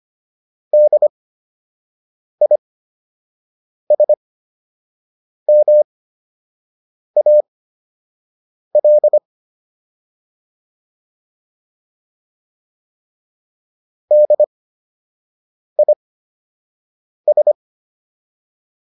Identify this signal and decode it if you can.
Morse code, heard as a keyed tone, about 25 wpm, spaced as for 6 wpm, 600 Hz. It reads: DISMAL DIS